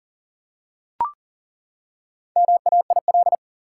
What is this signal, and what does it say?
Morse code, heard as a keyed tone, about 40 wpm, 700 Hz. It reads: MAIL
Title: Four Letter Words - 40wpm